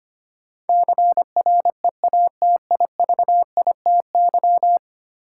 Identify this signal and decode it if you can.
Morse code, heard as a keyed tone, about 25 wpm, 700 Hz. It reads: CREATIVITY
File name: Ten Letter Words - 25wpm